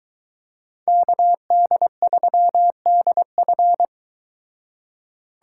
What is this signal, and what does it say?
Morse code, heard as a keyed tone, about 22 wpm, 700 Hz. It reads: KD3DF